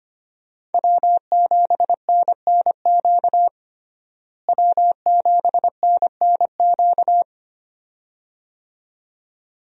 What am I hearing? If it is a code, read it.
Morse code, heard as a keyed tone, about 25 wpm, 700 Hz. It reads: W7NNQ W7NNQ
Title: USA Call Signs - 5 Characters - Repeated 2x - 25wpm